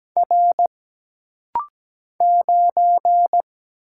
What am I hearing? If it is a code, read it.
Morse code, heard as a keyed tone, about 17 wpm, 700 Hz. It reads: R 9
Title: Single Letter-Number - Rapid-Fire 17wpm